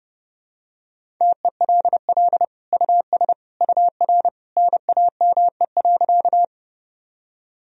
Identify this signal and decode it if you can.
Morse code, heard as a keyed tone, about 32 wpm, 700 Hz. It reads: TELL US UR NAME.
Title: Sentences from Top 200 Words and Abbreviations 30wpm